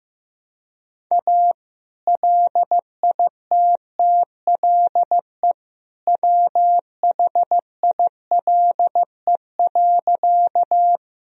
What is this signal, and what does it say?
Morse code, heard as a keyed tone, about 15 wpm, 700 Hz. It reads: A LITTLE WHILE.
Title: Sentences from Top 1000 Words - Review Part 1 - 15wpm